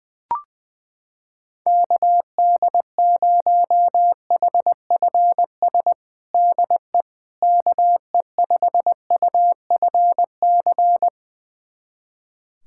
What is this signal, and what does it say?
Morse code, heard as a keyed tone, about 21 wpm, 700 Hz. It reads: KD0HFS DE KE5UFC
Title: USA Callsign DE Callsign - 6 Characters - 20wpm